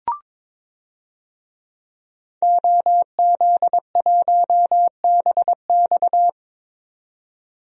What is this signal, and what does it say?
Morse code, heard as a keyed tone, about 22 wpm, 700 Hz. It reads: OZ1BX
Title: DXCC-100 Call Signs - 5 Characters - 22wpm